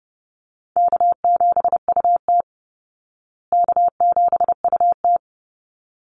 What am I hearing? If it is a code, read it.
Morse code, heard as a keyed tone, about 30 wpm, 700 Hz. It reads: K7UT K7UT
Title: Easy Call Signs Repeated 2x - 30wpm